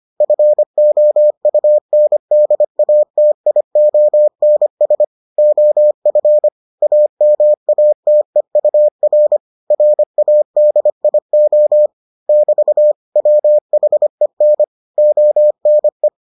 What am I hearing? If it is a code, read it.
Morse code, heard as a keyed tone, about 25 wpm, 600 Hz. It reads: FOUNDATIONS OF AMATEUR RADIO <BT> WHEN ONE